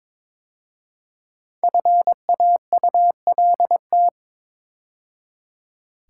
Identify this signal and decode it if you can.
Morse code, heard as a keyed tone, about 22 wpm, 700 Hz. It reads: FAULT
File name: 5 Letter Words - First 3 Letters Given - 22wpm